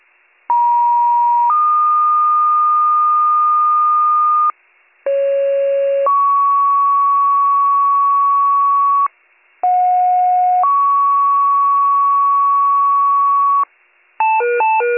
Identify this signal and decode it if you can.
Two-tone page, heard as a one-second tone, then a three-second tone, about 950 then 1220 Hz, about 550 then 1120 Hz, about 730 then 1120 Hz. It is heard from 0.5 to 13.6 s.